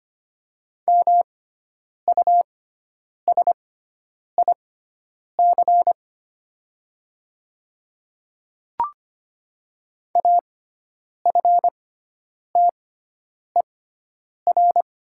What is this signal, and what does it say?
Morse code, heard as a keyed tone, about 25 wpm, spaced as for 9 wpm, 700 Hz. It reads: MUSIC AFTER
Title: 6x Character Spacing - Top 300 Words - 25wpm